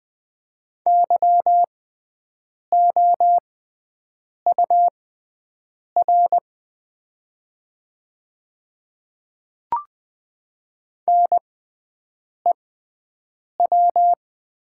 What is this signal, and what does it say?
Morse code, heard as a keyed tone, about 20 wpm, spaced as for 7 wpm, 700 Hz. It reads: YOUR NEW
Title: 6x Character Spacing - Top 300 Words - 20wpm